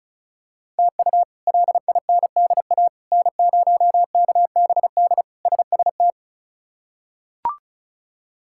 Morse code, heard as a keyed tone, about 36 wpm, 700 Hz. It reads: TU LINDA N0KBD SST
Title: K1USN SST QSO Practice - 35wpm